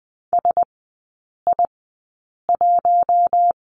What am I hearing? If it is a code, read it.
Morse code, heard as a keyed tone, about 20 wpm, spaced as for 8 wpm, 690 Hz. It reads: SI1
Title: 081 Lesson 14e - Number 2 - Three Characters 20wpm